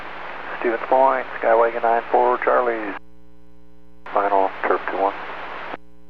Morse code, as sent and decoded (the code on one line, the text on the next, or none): none